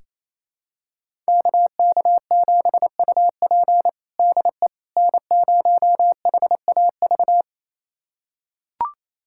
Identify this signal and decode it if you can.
Morse code, heard as a keyed tone, about 27 wpm, 700 Hz. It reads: KK7UP DE N0HAV
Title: USA Callsign DE Callsign - 5 Characters - 28wpm